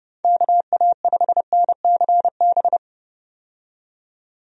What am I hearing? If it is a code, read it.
Morse code, heard as a keyed tone, about 31 wpm, 700 Hz. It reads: KA5NCB
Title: USA Call Signs - 6 Characters - 30wpm